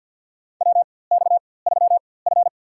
Morse code, heard as a keyed tone, about 47 wpm, spaced as for 23 wpm, 700 Hz. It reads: WX3F
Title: USA Call Signs - 4x Character Spacing - 50wpm